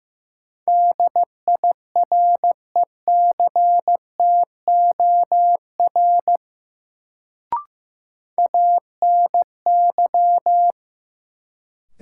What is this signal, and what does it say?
Morse code, heard as a keyed tone, about 15 wpm, 700 Hz. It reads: DIRECTOR ANY
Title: Sets of 3 Words - New Word Drill - 15wpm